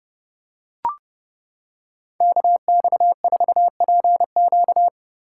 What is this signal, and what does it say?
Morse code, heard as a keyed tone, about 30 wpm, 700 Hz. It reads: KX4PQ
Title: K1USN SST QSO Practice - 30wpm